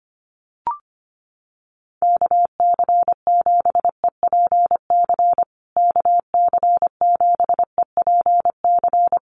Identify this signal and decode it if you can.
Morse code, heard as a keyed tone, about 25 wpm, 700 Hz. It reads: KC7EPC KC7EPC
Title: USA Call Signs Repeated 3x 25wpm